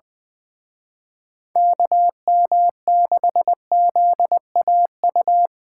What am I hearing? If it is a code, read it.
Morse code, heard as a keyed tone, about 24 wpm, 700 Hz. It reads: KM6ZAU
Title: Speed Racing - USA 2x3 Call Signs 20wpm